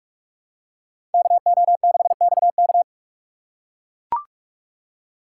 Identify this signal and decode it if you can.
Morse code, heard as a keyed tone, about 44 wpm, 700 Hz. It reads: KO6XK